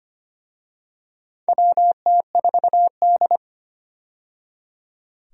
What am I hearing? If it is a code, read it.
Morse code, heard as a keyed tone, about 25 wpm, 700 Hz. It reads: WT4D
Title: USA Call Signs - 4 Characters - 25wpm